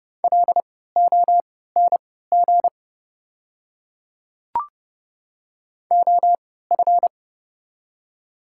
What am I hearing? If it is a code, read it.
Morse code, heard as a keyed tone, about 30 wpm, spaced as for 17 wpm, 700 Hz. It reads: LONG OF
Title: 3x Character Spacing - Top 100 Words - 30wpm